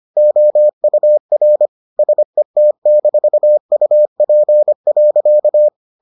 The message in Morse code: OUR SET-UP.

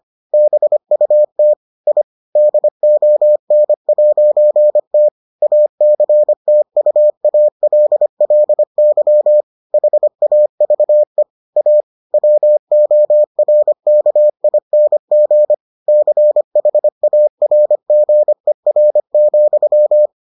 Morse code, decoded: BUT I DON'T ACTUALLY HAVE A WORKING CHARGER,

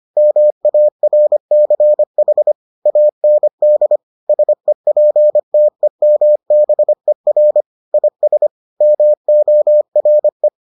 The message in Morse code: MARCH AND SEPTEMBER IS MORE

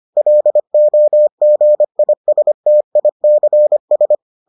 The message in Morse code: LOGISTICS

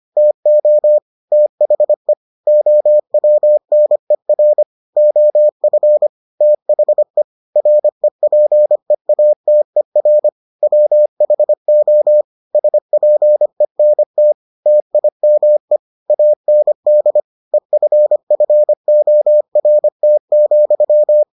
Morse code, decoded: TO THE OWNER OF THE REPEATER WHO SPENT TIME AND EFFORT,

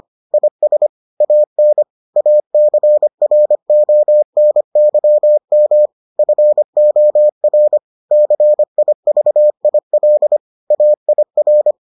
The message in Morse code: IS AN ACRONYM FOR CIVIL AIR

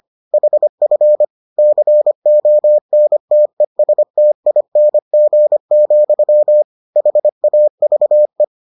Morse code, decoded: HF CONTESTING, HAVE

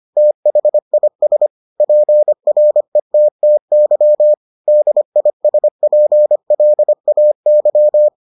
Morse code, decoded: THIS PRETTY DISPLAY

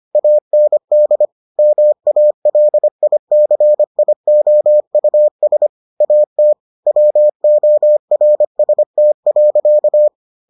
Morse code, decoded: AND MALICIOUS AT WORST.

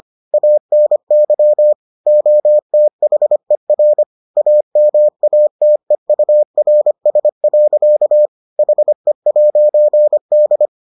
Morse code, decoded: ANY OTHER AMATEURS. HE'D

